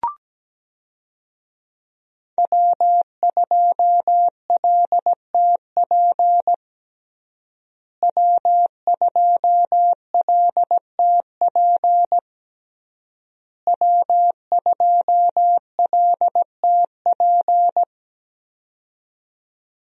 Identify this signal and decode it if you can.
Morse code, heard as a keyed tone, about 17 wpm, 700 Hz. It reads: W2LTP W2LTP W2LTP
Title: USA Call Signs - 5 Characters - Repeated 3x - 17wpm